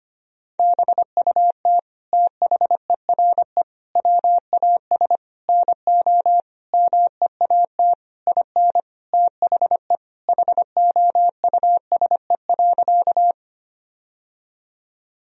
Morse code, heard as a keyed tone, about 25 wpm, 700 Hz. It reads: BUT THERE WAS NO MEAT IN THE HOUSE.
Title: Sentences from Top 1000 Words - Review Part 3 - 25wpm